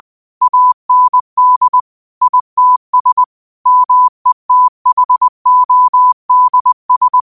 Morse code, heard as a keyed tone, about 20 wpm, 1000 Hz. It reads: AND ITS METHODS
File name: qotdfiles / 20wpm